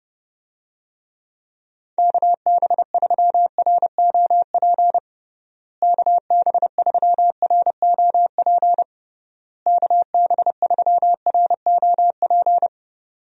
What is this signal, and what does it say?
Morse code, heard as a keyed tone, about 32 wpm, 700 Hz. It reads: KB3ROP KB3ROP KB3ROP
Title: USA Call Signs - 6 Characters - Repeated 3x - 30wpm